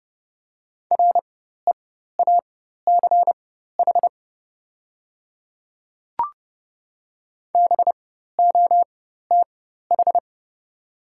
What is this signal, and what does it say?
Morse code, heard as a keyed tone, about 31 wpm, spaced as for 14 wpm, 690 Hz. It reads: REACH BOTH
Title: 4x Character Spacing - Top 500 Words - 30wpm